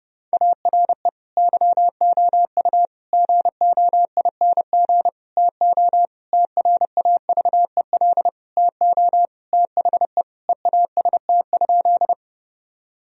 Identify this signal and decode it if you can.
Morse code, heard as a keyed tone, about 30 wpm, 700 Hz. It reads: ARE YOU GOING TO TRAVEL TO THE EAST?
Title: Sentences from Top 1000 Words - Review Part 3 - 30wpm